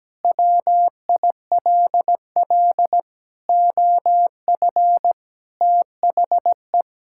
Morse code, heard as a keyed tone, about 17 wpm, 700 Hz. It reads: WILL OF THE
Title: Sentences from Top 1000 Words - Review Part 5 - 17wpm